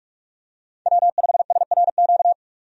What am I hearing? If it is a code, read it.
Morse code, heard as a keyed tone, about 46 wpm, 700 Hz. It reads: W5SRQ
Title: USA Call Signs - 5 Characters - 45wpm